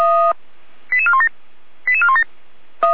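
Five-tone selective call (ZVEI), heard as quick runs of five tones, tones of about 70 ms. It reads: 88417 88417
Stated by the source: ZVEI five-tone selective call 88287 received on an SDR, logged by Feuerwehr Orlamünde (feuerwehr-orlamuende.org).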